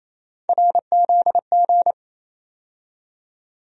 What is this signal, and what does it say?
Morse code, heard as a keyed tone, about 28 wpm, 700 Hz. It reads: RZG
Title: USA Call Sign Postfixes - 3 Letters 28wpm